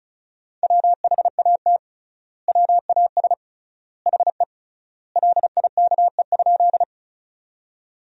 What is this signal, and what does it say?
Morse code, heard as a keyed tone, about 34 wpm, 700 Hz. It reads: WHAT WAS HE LIKE?
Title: 3x Word-Spacing - Sentences from Top 100 Words - 35wpm